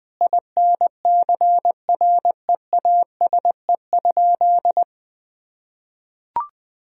Morse code, heard as a keyed tone, about 20 wpm, 700 Hz. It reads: INCREASE?
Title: Sentences from Top 1000 Words - Review Part 4 - 20wpm